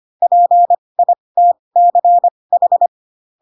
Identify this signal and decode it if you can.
Morse code, heard as a keyed tone, about 25 wpm, spaced as for 20 wpm, 700 Hz. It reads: PITCH
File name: Pinocchio / Pinocchio ch6 15 25 ik1hge